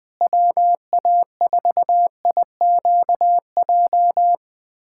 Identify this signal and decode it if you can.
Morse code, heard as a keyed tone, about 20 wpm, 700 Hz. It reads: WA4IQJ
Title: USA Callsign DE Callsign - 6 Characters - 20wpm